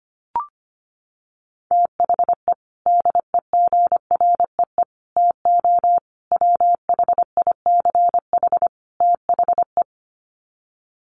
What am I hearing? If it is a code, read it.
Morse code, heard as a keyed tone, about 25 wpm, 700 Hz. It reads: THE DEGREE TO WHICH THE
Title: Sets of 5 Words Encore2 25wpm